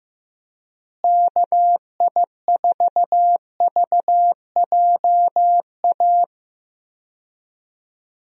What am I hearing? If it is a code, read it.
Morse code, heard as a keyed tone, about 15 wpm, 700 Hz. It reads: KI4VJA
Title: USA Call Signs - 6 Characters - Repeated 2x - 15wpm